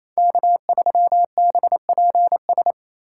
Morse code, heard as a keyed tone, about 36 wpm, 700 Hz. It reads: K3BPS